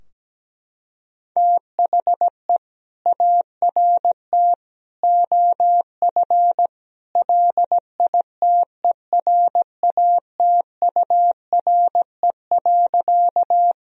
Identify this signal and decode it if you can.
Morse code, heard as a keyed tone, about 17 wpm, 700 Hz. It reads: THE ART OF LITERATURE.
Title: Sentences from Top 1000 Words - Review Part 3 - 17wpm